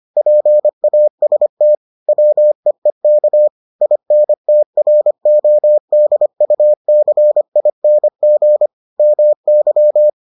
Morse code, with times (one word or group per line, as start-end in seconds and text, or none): 0.2-1.8 PAST
2.1-3.5 WEEK
3.8-8.7 INTRODUCING
9.0-10.1 MY